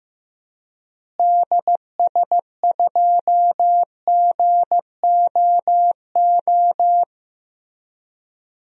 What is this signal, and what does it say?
Morse code, heard as a keyed tone, about 15 wpm, 700 Hz. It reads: DS2GOO
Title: DXCC-100 Call Signs - 6 Characters - Repeated 3x - 15wpm